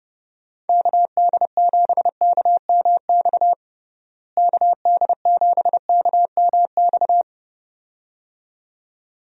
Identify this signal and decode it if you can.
Morse code, heard as a keyed tone, about 32 wpm, 700 Hz. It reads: KD7KMX KD7KMX
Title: USA Call Signs - 6 Characters - Repeated 2x - 30wpm